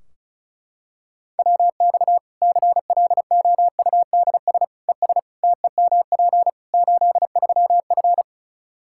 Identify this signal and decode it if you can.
Morse code, heard as a keyed tone, about 34 wpm, 700 Hz. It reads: WX CLOUDS ES TEMP 83F
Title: Practice QSO - Exchange 2 - Info - 35wpm